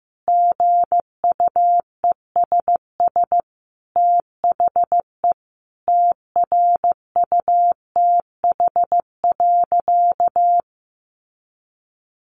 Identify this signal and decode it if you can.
Morse code, heard as a keyed tone, about 15 wpm, 700 Hz. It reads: GUESS THE TRUTH.
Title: Sentences from Top 1500 Words 15wpm